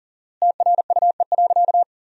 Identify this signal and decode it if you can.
Morse code, heard as a keyed tone, about 37 wpm, 700 Hz. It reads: TRUE.
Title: 4x Word-Spacing - Sentences from Top 400 Words - 40wpm